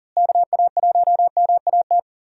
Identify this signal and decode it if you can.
Morse code, heard as a keyed tone, about 40 wpm, 700 Hz. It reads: KA1MAT
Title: USA Call Signs - 6 Characters - Repeated 3x - 40wpm